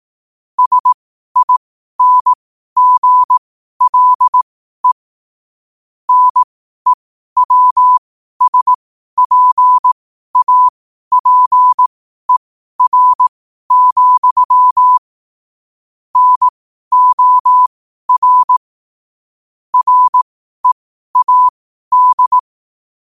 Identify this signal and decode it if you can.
Morse code, heard as a keyed tone, about 18 wpm, spaced as for 12 wpm, 1000 Hz. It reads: SINGLE NEWSPAPER, NOR READ
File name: qotdfiles / 12wpm